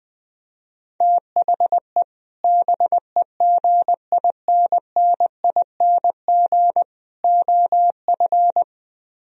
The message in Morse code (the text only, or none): THE BEGINNING OF